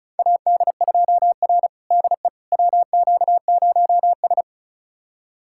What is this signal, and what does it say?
Morse code, heard as a keyed tone, about 36 wpm, 700 Hz. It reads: AD2R DE WQ0S
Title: USA Callsign DE Callsign - 4 Characters - 35wpm